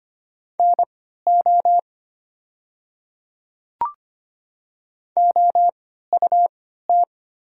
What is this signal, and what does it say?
Morse code, heard as a keyed tone, about 25 wpm, spaced as for 14 wpm, 690 Hz. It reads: NO OUT
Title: 3x Character Spacing - Top 200 Words - 25wpm